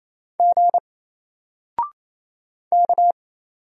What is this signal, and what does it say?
Morse code, heard as a keyed tone, about 27 wpm, 700 Hz. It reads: G K
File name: Single-Letter-Number-28wpm